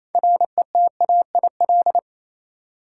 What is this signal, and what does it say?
Morse code, heard as a keyed tone, about 28 wpm, 700 Hz. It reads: RETAIL
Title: Practice QSO - Exchange 3 - More Info - 28wpm